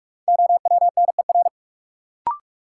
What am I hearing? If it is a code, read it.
Morse code, heard as a keyed tone, about 47 wpm, 700 Hz. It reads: OWNER
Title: Sentences from Top 1600 Words - New Word Drill - 45wpm